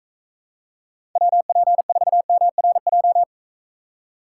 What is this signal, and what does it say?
Morse code, heard as a keyed tone, about 43 wpm, 700 Hz. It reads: WP4MRJ